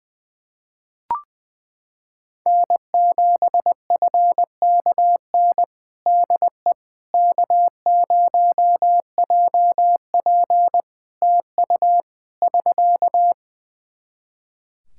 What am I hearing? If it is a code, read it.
Morse code, heard as a keyed tone, about 21 wpm, 700 Hz. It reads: N7FKN DE K0JP TU <SK>
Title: Practice QSO - Exchange 4 - Ending - 20wpm